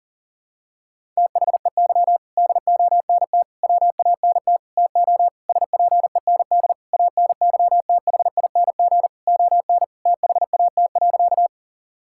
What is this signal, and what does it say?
Morse code, heard as a keyed tone, about 40 wpm, 700 Hz. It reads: THEY DONT WANT TO SPEND ANYTHING ON THAT.
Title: Sentences from Top 1000 Words - Review Part 5 - 40wpm